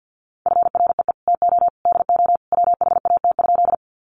Morse code, heard as a keyed tone, about 50 wpm, 700 Hz. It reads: FREE TO DO WHAT?
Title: Sentences from Top 500 Words 50wpm